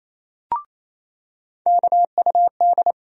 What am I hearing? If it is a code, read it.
Morse code, heard as a keyed tone, about 28 wpm, 700 Hz. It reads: KUD
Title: USA Call Sign Postfixes - 3 Letters 28wpm